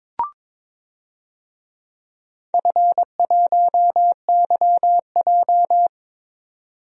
Morse code, heard as a keyed tone, about 22 wpm, 700 Hz. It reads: F1YJ
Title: DXCC-100 Call Signs - 4 Characters - 22wpm